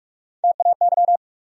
Morse code, heard as a keyed tone, about 46 wpm, 700 Hz. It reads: TAY
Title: USA Call Sign Postfixes - 3 Letters 45wpm